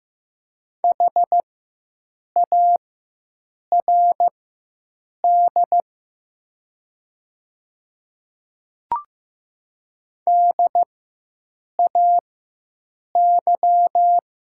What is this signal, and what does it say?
Morse code, heard as a keyed tone, about 15 wpm, spaced as for 7 wpm, 700 Hz. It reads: HARD DAY
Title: 4x Character Spacing - Top 300 Words - 15wpm